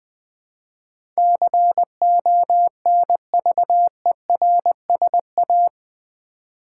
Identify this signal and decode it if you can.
Morse code, heard as a keyed tone, about 20 wpm, 700 Hz. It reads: CONVERSA
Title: Build A Word - 4 Letters At A Time - 12-Letter Words - 20wpm